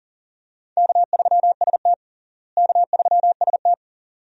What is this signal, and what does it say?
Morse code, heard as a keyed tone, about 37 wpm, 700 Hz. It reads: K3ST K3ST
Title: USA Call Signs - 4 Characters - Repeated 2x - 40wpm